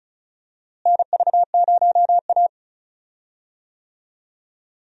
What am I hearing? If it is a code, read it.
Morse code, heard as a keyed tone, about 36 wpm, 700 Hz. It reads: NV0A
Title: USA Call Signs - 4 Characters - Repeated 2x - 35wpm